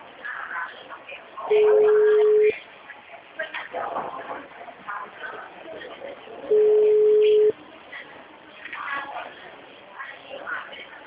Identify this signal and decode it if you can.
Ringback tone (Europe, 425 Hz), with speech.